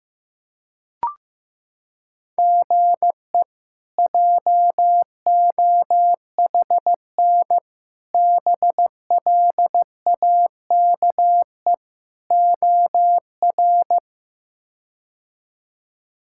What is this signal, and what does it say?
Morse code, heard as a keyed tone, about 15 wpm, 700 Hz. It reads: GE JOHN BLAKE OR
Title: K1USN SST QSO Practice - 15wpm